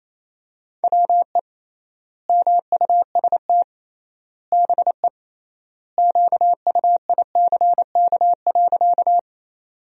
Morse code, heard as a keyed tone, about 28 wpm, 700 Hz. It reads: WE MUST BE QUICK.
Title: 3x Word-Spacing - Sentences from Top 500 Words - 28wpm